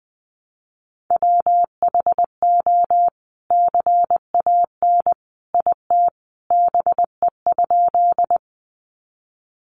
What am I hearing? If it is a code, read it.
Morse code, heard as a keyed tone, about 20 wpm, 700 Hz. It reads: WHO CAN IT BE?
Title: Sentences from Top 100 Words 20wpm